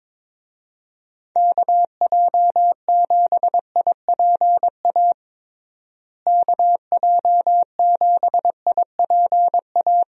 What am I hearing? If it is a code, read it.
Morse code, heard as a keyed tone, about 22 wpm, 700 Hz. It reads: KJ7IPA KJ7IPA
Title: USA Call Signs - 6 Characters - Repeated 2x - 22wpm